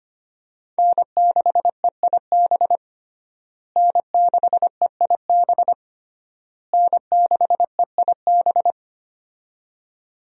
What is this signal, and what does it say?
Morse code, heard as a keyed tone, about 25 wpm, 700 Hz. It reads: N6EIB N6EIB N6EIB
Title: USA Call Signs - 5 Characters - Repeated 3x - 25wpm